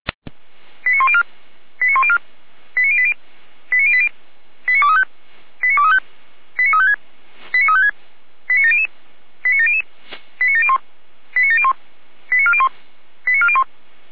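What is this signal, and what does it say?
Five-tone selective call (ZVEI), heard as quick runs of five tones, tones of about 70 ms. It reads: 89104 89104 89080 89080 89235 89235 89356 89356 89700 89700 89701 89701 89401 89401
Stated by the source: ZVEI five-tone call 89094 received by SDR.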